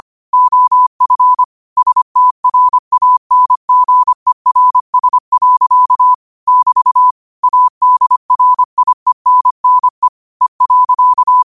Morse code, heard as a keyed tone, about 25 wpm, 1000 Hz. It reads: OF STRANGERS. <BT> ADRIENNE E.
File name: 25wpm